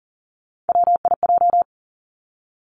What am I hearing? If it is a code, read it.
Morse code, heard as a keyed tone, about 40 wpm, 700 Hz. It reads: WIJ
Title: Three Challenging Letter-Number Combinations 40wpm